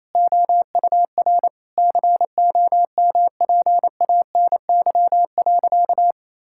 Morse code, heard as a keyed tone, about 28 wpm, 700 Hz. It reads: OUR COMPANY.
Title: Sentences from Top 1000 Words - Review Part 5 - 28wpm